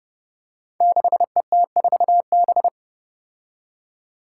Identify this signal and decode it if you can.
Morse code, heard as a keyed tone, about 32 wpm, 700 Hz. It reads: 6ET4B
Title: Five Challenging Letter-Number Combinations 30wpm